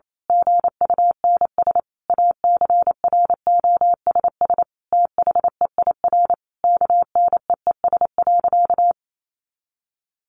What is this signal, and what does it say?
Morse code, heard as a keyed tone, about 28 wpm, 700 Hz. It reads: GUNS ACROSS THEIR KNEES.
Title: A Sound Of Thunder 28wpm